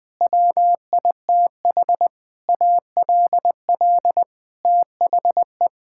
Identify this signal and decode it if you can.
Morse code, heard as a keyed tone, about 20 wpm, 700 Hz. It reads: WITH ALL THE
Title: Sentences with 9-Word Phrases - 20wpm